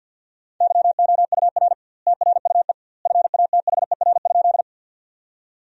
Morse code, heard as a keyed tone, about 47 wpm, 700 Hz. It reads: YOUR TRUE FATHER?
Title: 2x Word-Spacing - Sentences from Top 400 Words - 50wpm